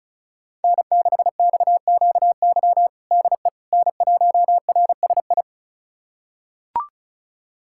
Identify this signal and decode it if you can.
Morse code, heard as a keyed tone, about 36 wpm, 700 Hz. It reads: N6XQY DE N1RSI
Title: USA Callsign DE Callsign - 5 Characters - 35wpm